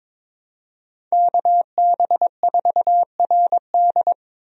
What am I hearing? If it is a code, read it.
Morse code, heard as a keyed tone, about 22 wpm, 700 Hz. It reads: KB4RD
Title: USA Call Signs - 5 Characters - Repeated 2x - 22wpm